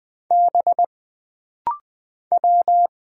Morse code, heard as a keyed tone, about 20 wpm, 700 Hz. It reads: B W
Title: Single Letters - Rapid-Fire 20wpm